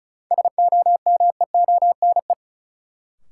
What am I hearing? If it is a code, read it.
Morse code, heard as a keyed tone, about 36 wpm, 700 Hz. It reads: SOMEONE
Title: Sentences from Top 600 Words - New Word Drill - 35wpm